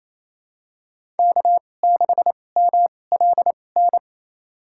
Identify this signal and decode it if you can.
Morse code, heard as a keyed tone, about 28 wpm, spaced as for 20 wpm, 700 Hz. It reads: K6MLN
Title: USA Call Signs - 5 Characters - 2x Character Spacing - 28wpm